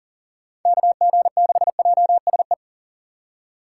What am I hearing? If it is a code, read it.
Morse code, heard as a keyed tone, about 40 wpm, 690 Hz. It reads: KG6JSE
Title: USA Call Signs - 4x Character Spacing - 40wpm